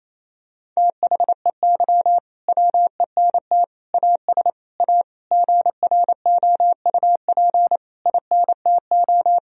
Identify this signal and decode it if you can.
Morse code, heard as a keyed tone, about 27 wpm, 700 Hz. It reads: THEY WENT AS A GROUP INTO
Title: Sentences from Top 1000 Words - Review Part 1 - 28wpm